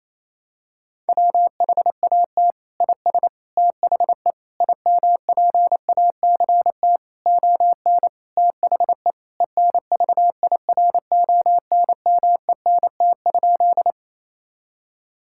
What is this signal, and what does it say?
Morse code, heard as a keyed tone, about 28 wpm, 700 Hz. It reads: WHAT IS THE IMPACT ON THE ENVIRONMENT?
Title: Sentences from Top 1000 Words - Review Part 6 - 28wpm